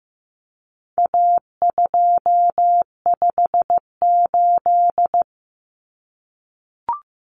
Morse code, heard as a keyed tone, about 15 wpm, 700 Hz. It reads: A258
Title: Common Cut Numbers - 4-Digit 15wpm